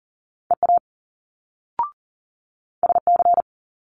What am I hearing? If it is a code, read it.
Morse code, heard as a keyed tone, about 42 wpm, 700 Hz. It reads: EA SC
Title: Multi-letter Phonemes Quick Fire 40wpm